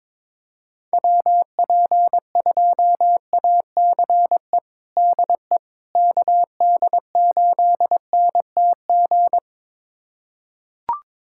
Morse code, heard as a keyed tone, about 22 wpm, 700 Hz. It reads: WP2ACE DE KD8NTG
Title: USA Callsign DE Callsign - 6 Characters - 22wpm